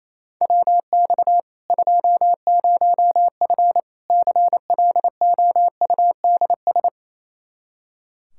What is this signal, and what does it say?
Morse code, heard as a keyed tone, about 28 wpm, 700 Hz. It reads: WX 20F CLOUDS
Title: Practice QSO - Exchange 2 - Info - 28wpm